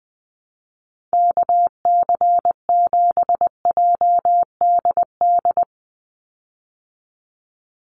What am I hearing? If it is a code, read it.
Morse code, heard as a keyed tone, about 19 wpm, 700 Hz. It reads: KC7JDD
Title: USA Call Signs 20wpm